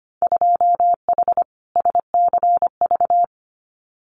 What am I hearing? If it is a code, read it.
Morse code, heard as a keyed tone, about 25 wpm, 700 Hz. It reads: 2H SCV
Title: Winter Field Day Exchanges Encore 25wpm